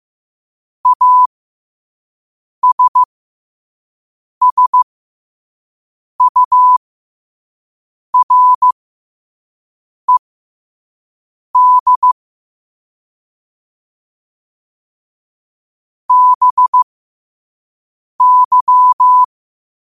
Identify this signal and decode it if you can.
Morse code, heard as a keyed tone, about 15 wpm, spaced as for 5 wpm, 1000 Hz. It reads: ASSURED BY
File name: qotdfiles / 5wpm